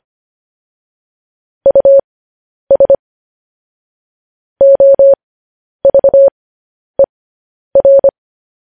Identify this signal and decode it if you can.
Morse code, heard as a keyed tone, about 25 wpm, spaced as for 10 wpm, 550 Hz. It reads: US OVER
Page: news headlines in morse code at 10 wpm